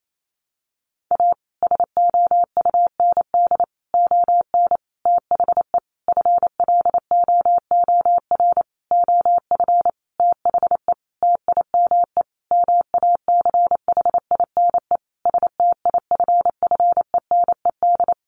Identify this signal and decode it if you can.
Morse code, heard as a keyed tone, about 28 wpm, 700 Hz. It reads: A SOUND ON THE FLOOR OF THE TIME MACHINE STIFFENED
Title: A Sound Of Thunder 28wpm